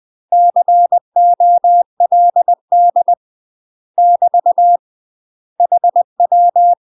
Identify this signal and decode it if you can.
Morse code, heard as a keyed tone, about 20 wpm, 700 Hz. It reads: COLD <BT> HW
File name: UK MASTER QSOs TRAINING / 20WPM / UK MASTER QSOs 0064